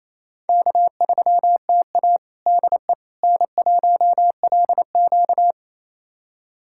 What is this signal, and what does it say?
Morse code, heard as a keyed tone, about 28 wpm, 700 Hz. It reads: K3TA DE N1LQ
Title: USA Callsign DE Callsign - 4 Characters - 28wpm